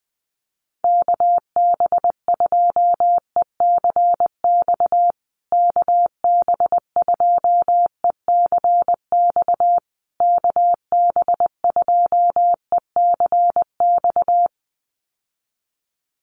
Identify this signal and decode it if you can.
Morse code, heard as a keyed tone, about 20 wpm, 700 Hz. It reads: KB2ECX KB2ECX KB2ECX
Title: USA Call Signs Repeated 3x 20wpm